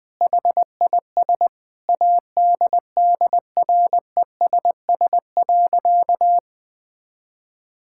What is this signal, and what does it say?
Morse code, heard as a keyed tone, about 20 wpm, 700 Hz. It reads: HIS ADDRESS.
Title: Sentences from Top 1000 Words - Review Part 6 - 20wpm